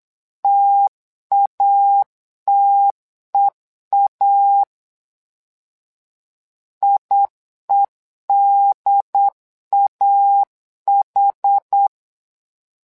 Morse code, heard as a keyed tone, about 8 wpm, 800 Hz. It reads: TATEA IEDAH
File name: LEZIONI / 2 UHDN / 2 UHD 8wpm 2258